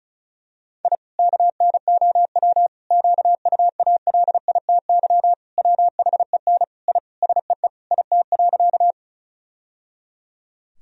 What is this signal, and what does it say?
Morse code, heard as a keyed tone, about 34 wpm, 700 Hz. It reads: I KNOW QUALITY WHEN I SEE IT.
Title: Sentences from Top 1000 Words - Review Part 4 - 35wpm